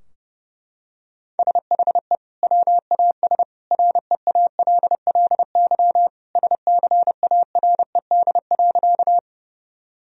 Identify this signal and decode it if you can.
Morse code, heard as a keyed tone, about 31 wpm, 700 Hz. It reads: SHE WAS REALLY SCARED.
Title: Sentences from Top 1000 Words - Review Part 4 - 30wpm